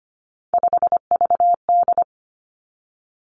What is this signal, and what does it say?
Morse code, heard as a keyed tone, about 25 wpm, 700 Hz. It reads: 5VD
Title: Three Challenging Letter-Number Combinations 25wpm